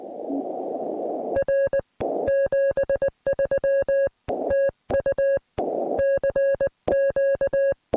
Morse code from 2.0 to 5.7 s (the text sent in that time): 73 TU